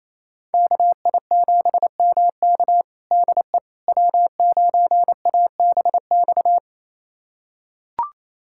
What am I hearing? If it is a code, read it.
Morse code, heard as a keyed tone, about 28 wpm, 700 Hz. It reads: KI7MK DE W9ABX
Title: USA Callsign DE Callsign - 5 Characters - 28wpm